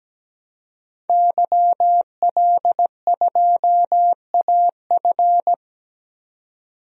Morse code, heard as a keyed tone, about 17 wpm, 700 Hz. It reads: YL2AF
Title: DXCC-100 Call Signs - 5 Characters - 17wpm